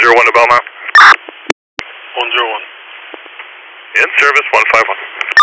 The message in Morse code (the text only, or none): none